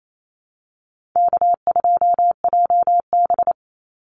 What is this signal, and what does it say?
Morse code, heard as a keyed tone, about 28 wpm, 700 Hz. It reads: K2JB
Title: NA SOTA Activator Callsigns 28wpm